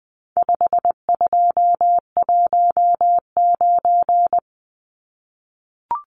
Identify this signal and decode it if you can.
Morse code, heard as a keyed tone, about 20 wpm, 700 Hz. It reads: 5219
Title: Numbers - 4-Digit 20wpm